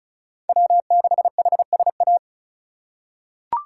Morse code, heard as a keyed tone, about 36 wpm, 700 Hz. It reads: W6HSA
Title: USA Call Signs - 5 Characters - 2x Character Spacing - 35wpm